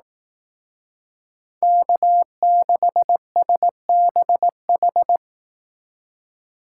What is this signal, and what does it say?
Morse code, heard as a keyed tone, about 18 wpm, 700 Hz. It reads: K6SBH
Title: Speed Racing - USA Call Signs 15wpm